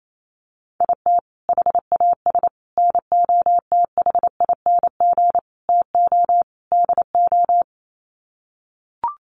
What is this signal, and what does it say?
Morse code, heard as a keyed tone, about 28 wpm, 700 Hz. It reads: IT HAS NOTHING TO DO
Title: Sets of 5 Words Encore 28wpm